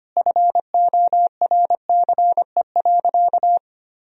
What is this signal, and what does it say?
Morse code, heard as a keyed tone, about 25 wpm, 700 Hz. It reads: FORCE.
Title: Sentences from Top 1000 Words - Review Part 4 - 25wpm